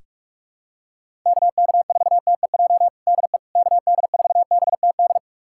Morse code, heard as a keyed tone, about 46 wpm, 700 Hz. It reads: KK4TEJ DE KD4BTD
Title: USA Callsign DE Callsign - 6 Characters - 45wpm